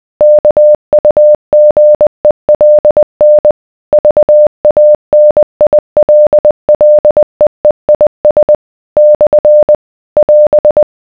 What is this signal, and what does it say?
Morse code, heard as a keyed tone, about 20 wpm, 600 Hz. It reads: KUGELN VANILLEEIS / <AS>